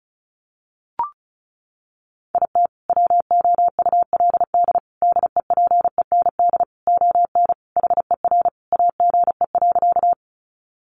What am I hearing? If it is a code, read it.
Morse code, heard as a keyed tone, about 34 wpm, 700 Hz. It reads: IT WOULD DEPEND ON HER AGE.
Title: Sentences from Top 1300 Words 35wpm